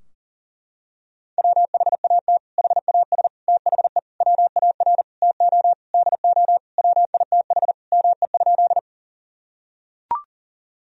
Morse code, heard as a keyed tone, about 40 wpm, 700 Hz. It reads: WHAT HAS THE WAR TO DO WITH ME?
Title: Sentences from Top 1000 Words - Review Part 2 - 40wpm